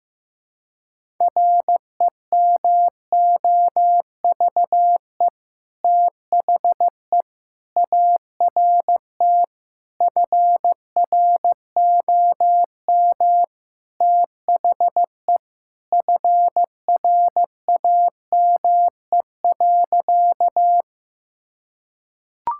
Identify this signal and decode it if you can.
Morse code, heard as a keyed tone, about 16 wpm, 700 Hz. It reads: REMOVE THE ART FROM THE FRAME.
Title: Sentences from Top 1000 Words - Review Part 6 - 15wpm